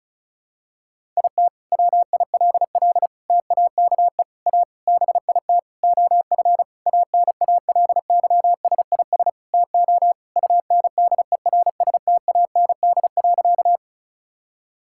Morse code, heard as a keyed tone, about 34 wpm, 700 Hz. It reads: IT WILL TAKE A BIT OF ANALYSIS TO UNDERSTAND.
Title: Sentences from Top 1000 Words - Review Part 5 - 35wpm